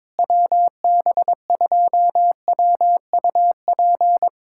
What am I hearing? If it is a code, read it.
Morse code, heard as a keyed tone, about 22 wpm, 700 Hz. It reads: WB2WUP
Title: USA Callsign DE Callsign - 6 Characters - 22wpm